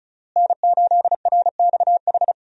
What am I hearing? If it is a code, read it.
Morse code, heard as a keyed tone, about 36 wpm, 700 Hz. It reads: N8RXH
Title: USA Call Signs - 5 Characters - Repeated 3x - 35wpm